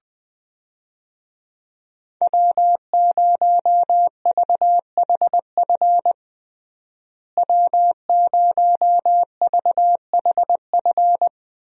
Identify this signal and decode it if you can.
Morse code, heard as a keyed tone, about 21 wpm, 700 Hz. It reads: W0VHF W0VHF
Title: USA Call Signs - 5 Characters - Repeated 3x - 20wpm